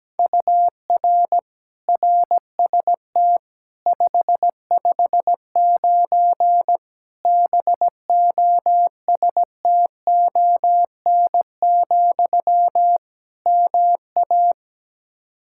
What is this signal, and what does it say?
Morse code, heard as a keyed tone, about 17 wpm, 700 Hz. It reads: UR RST 559 BOSTON, MA
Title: Practice QSO - Exchange 1 - Beginning - 17wpm